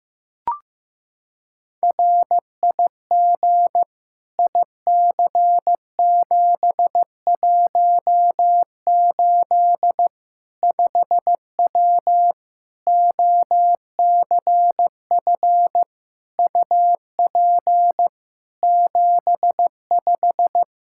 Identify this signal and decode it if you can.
Morse code, heard as a keyed tone, about 16 wpm, 700 Hz. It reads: RIG IC718 5W OCF UP 75
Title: K1USN SST QSO Practice 15wpm